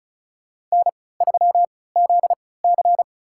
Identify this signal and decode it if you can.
Morse code, heard as a keyed tone, about 36 wpm, spaced as for 20 wpm, 700 Hz. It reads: N3ZC
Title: USA Call Signs - 4 Characters - 3x Character Spacing - 35wpm